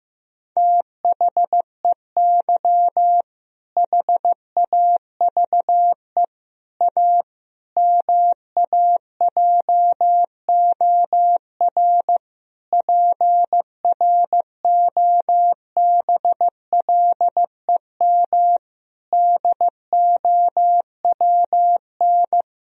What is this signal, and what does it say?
Morse code, heard as a keyed tone, about 16 wpm, 700 Hz. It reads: THEY HAVE A MAJOR PROBLEM DOWN
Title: Sentences from Top 1000 Words - Review Part 4 - 15wpm